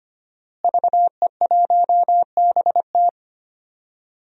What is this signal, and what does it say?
Morse code, heard as a keyed tone, about 25 wpm, 700 Hz. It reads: VE1BT
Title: DXCC-100 Call Signs - 5 Characters - Repeated 2x - 25wpm